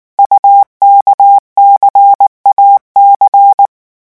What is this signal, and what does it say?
Morse code, heard as a keyed tone, about 19 wpm, 800 Hz. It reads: UKCAC
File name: morse mp3 / revision2-vitesse 18 mots